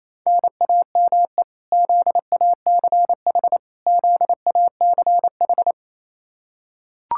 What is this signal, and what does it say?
Morse code, heard as a keyed tone, about 28 wpm, 700 Hz. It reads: NAME ZACH ZACH
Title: Practice QSO - Exchange 1 - Beginning - 28wpm